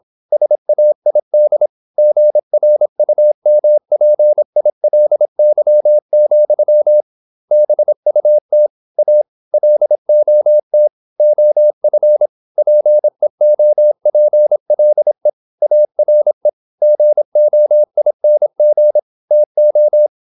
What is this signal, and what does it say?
Morse code, heard as a keyed tone, about 27 wpm, 600 Hz. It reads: SAID GRUMPILY, BUT A LOT OF PEOPLE ARE GOING TO
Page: Never 26wpm | CW med Gnister